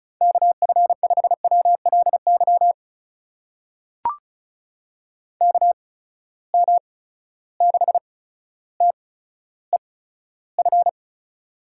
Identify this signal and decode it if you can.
Morse code, heard as a keyed tone, about 36 wpm, spaced as for 15 wpm, 700 Hz. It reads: KF5WLY KM6TEF
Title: USA Call Signs - 6 Characters - 8x Character Spacing - 35wpm